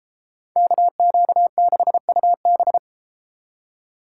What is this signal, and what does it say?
Morse code, heard as a keyed tone, about 32 wpm, 700 Hz. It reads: KQ6UB